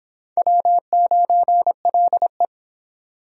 Morse code, heard as a keyed tone, about 21 wpm, 700 Hz. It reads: W9LE